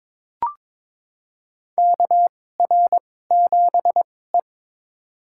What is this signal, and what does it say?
Morse code, heard as a keyed tone, about 22 wpm, spaced as for 16 wpm, 700 Hz. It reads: KR7E
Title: USA Call Signs - 2x Character Spacing - 22wpm